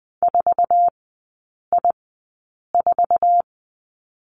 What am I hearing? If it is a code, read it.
Morse code, heard as a keyed tone, about 20 wpm, spaced as for 8 wpm, 700 Hz. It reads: 4I4
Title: 047 Lesson 9e - Number 4 - Three Characters 20wpm